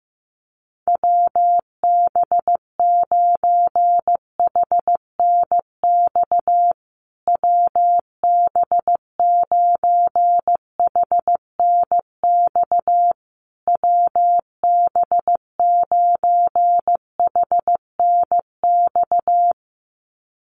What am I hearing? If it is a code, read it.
Morse code, heard as a keyed tone, about 15 wpm, 700 Hz. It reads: WB9HNX WB9HNX WB9HNX
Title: USA Call Signs Repeated 3x 15wpm